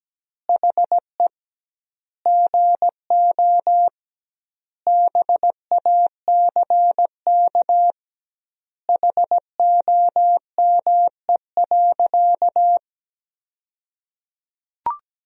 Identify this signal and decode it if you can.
Morse code, heard as a keyed tone, about 17 wpm, 700 Hz. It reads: HE GO BACK HOME.
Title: 2x Word-Spacing - Sentences from Top 500 Words - 17wpm